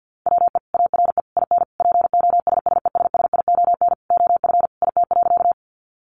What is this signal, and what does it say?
Morse code, heard as a keyed tone, about 50 wpm, 700 Hz. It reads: WE ARE IN POSSESSION OF IT.
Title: Sentences from Top 1400 Words 50wpm